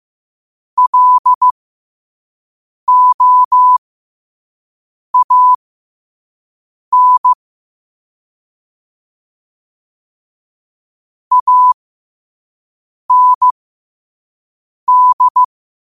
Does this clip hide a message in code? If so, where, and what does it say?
Yes: Morse code from 0.8 to 15.4 s, heard as a keyed tone, about 15 wpm, spaced as for 5 wpm, 1000 Hz. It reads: LOAN AND